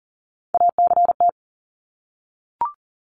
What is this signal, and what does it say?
Morse code, heard as a keyed tone, about 37 wpm, 700 Hz. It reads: ACT
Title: Top 300 Words Repeated 40wpm